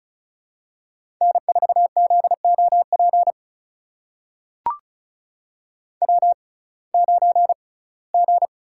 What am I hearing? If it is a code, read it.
Morse code, heard as a keyed tone, about 36 wpm, spaced as for 21 wpm, 700 Hz. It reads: N4ZOP W9G
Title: USA Call Signs - 6x Character Spacing - 35wpm